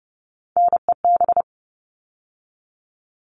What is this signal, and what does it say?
Morse code, heard as a keyed tone, about 30 wpm, 700 Hz. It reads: NEB
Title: All Cut Numbers - 3-Digit 30wpm